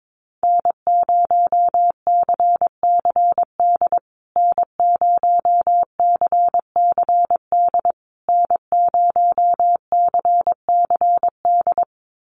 Morse code, heard as a keyed tone, about 22 wpm, 700 Hz. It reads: N0CCD N0CCD N0CCD
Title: USA Call Signs Repeated 3x 22wpm